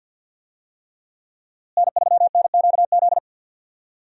Morse code, heard as a keyed tone, about 52 wpm, 700 Hz. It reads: N3NQZ
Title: USA Call Signs - 5 Characters - Repeated 3x - 50wpm